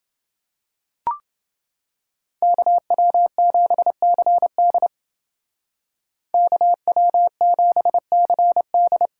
Morse code, heard as a keyed tone, about 28 wpm, 700 Hz. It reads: KW7CD KW7CD